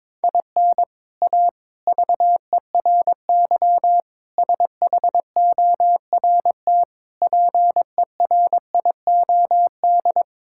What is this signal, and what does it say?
Morse code, heard as a keyed tone, about 22 wpm, 700 Hz. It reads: IN A VERY SHORT PERIOD
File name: practice / 22wpm / Sets of 7 Words - 22wpm